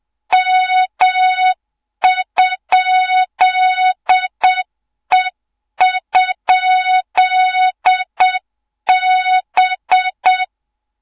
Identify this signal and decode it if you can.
Morse code, heard as a keyed tone, about 7 wpm, 3700 Hz, with noise. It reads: M?E?B
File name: Morze-HA2MN / hang / 24 GYAKORLAT